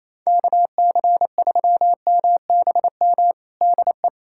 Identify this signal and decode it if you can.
Morse code, heard as a keyed tone, about 28 wpm, 700 Hz. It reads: KC3MBM DE
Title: USA Callsign DE Callsign 28wpm